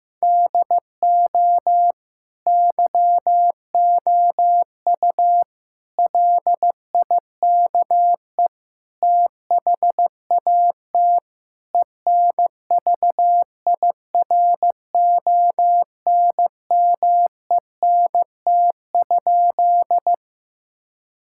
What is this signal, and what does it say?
Morse code, heard as a keyed tone, about 15 wpm, 700 Hz. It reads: DO YOU LIKE THAT ENVIRONMENT?
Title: Sentences from Top 1000 Words - Review Part 5 - 15wpm